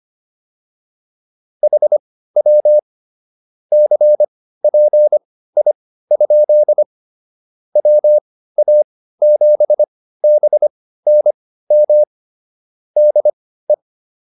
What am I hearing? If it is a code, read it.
Morse code, heard as a keyed tone, about 25 wpm, spaced as for 15 wpm, 600 Hz. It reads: HW CPI? WA7BNM DE